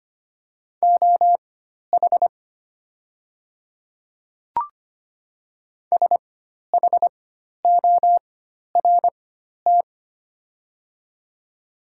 Morse code, heard as a keyed tone, about 25 wpm, spaced as for 12 wpm, 690 Hz. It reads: OH SHORT
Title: 4x Character Spacing - Top 500 Words - 25wpm